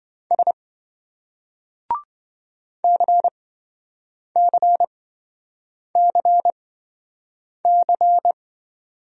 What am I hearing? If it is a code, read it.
Morse code, heard as a keyed tone, about 27 wpm, spaced as for 7 wpm, 700 Hz. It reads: S CCCC